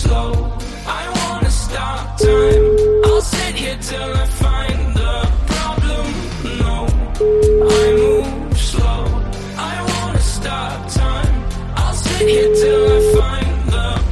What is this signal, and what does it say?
Ringback tone (Europe, 425 Hz), with noise.